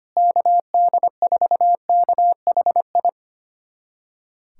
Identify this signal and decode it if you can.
Morse code, heard as a keyed tone, about 25 wpm, 700 Hz. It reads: KD4KHI
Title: USA Callsign DE Callsign - 6 Characters - 25wpm